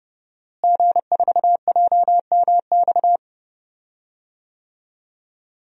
Morse code, heard as a keyed tone, about 30 wpm, 700 Hz. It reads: G4JMX
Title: DXCC-100 Call Signs - 5 Characters - Repeated 3x - 30wpm